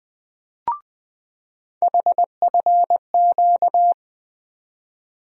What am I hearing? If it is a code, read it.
Morse code, heard as a keyed tone, about 20 wpm, 700 Hz. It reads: HFQ
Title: USA Call Sign Postfixes - 3 Letters 20wpm